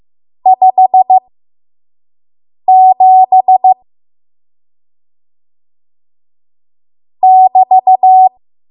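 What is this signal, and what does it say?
Morse code, heard as a keyed tone, about 15 wpm, spaced as for 5 wpm, 750 Hz. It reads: 57 <BT>
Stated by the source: 5 WPM Morse code practice.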